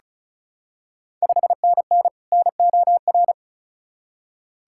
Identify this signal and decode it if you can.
Morse code, heard as a keyed tone, about 34 wpm, 700 Hz. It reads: 5NN NOR
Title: Ontario QSO Party Exchange 35wpm